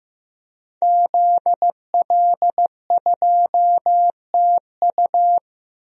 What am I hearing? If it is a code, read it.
Morse code, heard as a keyed tone, about 16 wpm, 700 Hz. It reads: ZL2TU
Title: DXCC-100 Call Signs - 5 Characters - Repeated 3x - 15wpm